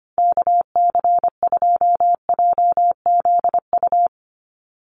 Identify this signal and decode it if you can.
Morse code, heard as a keyed tone, about 25 wpm, 700 Hz. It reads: KC2JZU
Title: USA Call Signs 25wpm